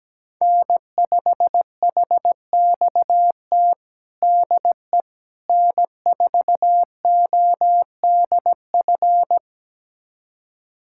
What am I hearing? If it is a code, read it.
Morse code, heard as a keyed tone, about 17 wpm, 700 Hz. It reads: N5HXT DE N4ODF
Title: USA Callsign DE Callsign - 5 Characters - 17wpm